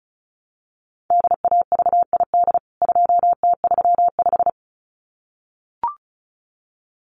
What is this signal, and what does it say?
Morse code, heard as a keyed tone, about 34 wpm, 700 Hz. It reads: DAVID 2T35
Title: CWT Exchange 35wpm